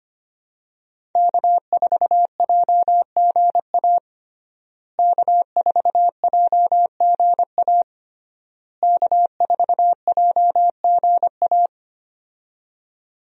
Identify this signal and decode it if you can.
Morse code, heard as a keyed tone, about 25 wpm, 700 Hz. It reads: K4JGA K4JGA K4JGA
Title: USA Call Signs - 5 Characters - Repeated 3x - 25wpm